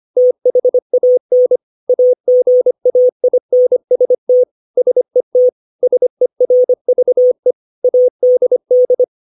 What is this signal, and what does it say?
Morse code, heard as a keyed tone, about 25 wpm, 500 Hz. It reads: THAN AGAINST SET SERVE ADD
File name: 632 words / 632 Words 25WPM